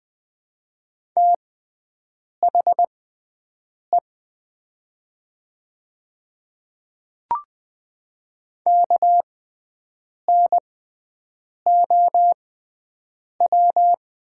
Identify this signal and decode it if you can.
Morse code, heard as a keyed tone, about 20 wpm, spaced as for 7 wpm, 700 Hz. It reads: THE KNOW
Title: 6x Character Spacing - Top 100 Words - 20wpm